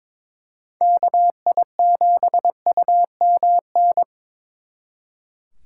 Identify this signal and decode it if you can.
Morse code, heard as a keyed tone, about 26 wpm, 700 Hz. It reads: KI7UMN
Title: Speed Racing - USA 2x3 Call Signs 22wpm